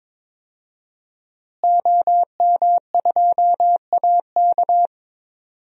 Morse code, heard as a keyed tone, about 22 wpm, 700 Hz. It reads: OM2AK
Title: DXCC-100 Call Signs - 5 Characters - Repeated 2x - 22wpm